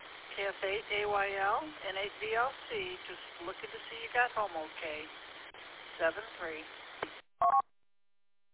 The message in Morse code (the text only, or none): none